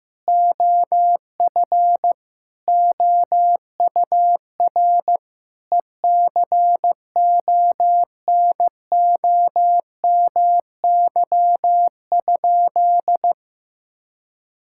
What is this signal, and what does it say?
Morse code, heard as a keyed tone, about 16 wpm, 700 Hz. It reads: OF OUR ECONOMY?
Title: Sentences from Top 1000 Words - Review Part 5 - 15wpm